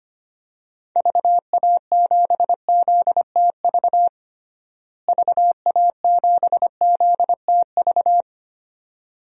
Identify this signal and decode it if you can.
Morse code, heard as a keyed tone, about 25 wpm, 700 Hz. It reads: VA7ZTV VA7ZTV
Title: DXCC-100 Call Signs - 6 Characters - Repeated 2x - 25wpm